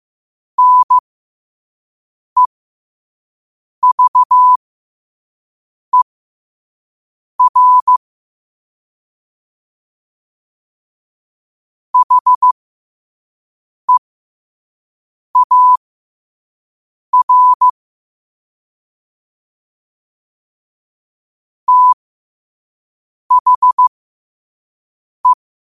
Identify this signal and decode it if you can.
Morse code, heard as a keyed tone, about 15 wpm, spaced as for 5 wpm, 1000 Hz. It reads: NEVER HEAR THE
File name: qotdfiles / 5wpm